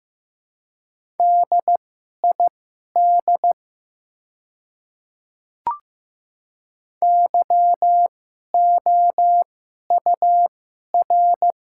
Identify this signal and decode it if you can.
Morse code, heard as a keyed tone, about 15 wpm, spaced as for 11 wpm, 690 Hz. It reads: DID YOUR
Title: 2x Character Spacing - Top 100 Words - 15wpm